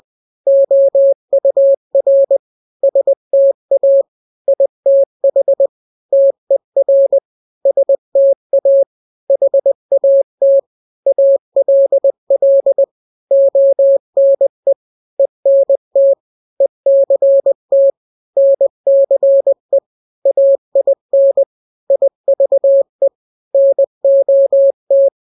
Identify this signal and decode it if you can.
Morse code, heard as a keyed tone, about 20 wpm, 550 Hz. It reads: OUR STA ITH TER STA HAT ALL ONE ENT ECT NCE AIN IVE NOT
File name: n-grams / 3-grams 20 18 WPM